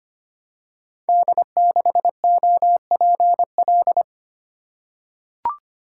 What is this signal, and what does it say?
Morse code, heard as a keyed tone, about 25 wpm, 700 Hz. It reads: D6OPL
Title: Code Groups - 5 Letter:Numbers - 25wpm